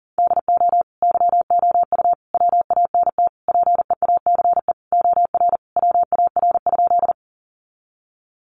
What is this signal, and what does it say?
Morse code, heard as a keyed tone, about 40 wpm, 700 Hz. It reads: DO YOU WANT PEACE OR WAR?
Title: Sentences from Top 1500 Words 40wpm